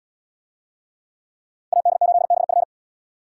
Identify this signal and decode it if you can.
Morse code, heard as a keyed tone, about 69 wpm, 700 Hz. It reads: AD8BF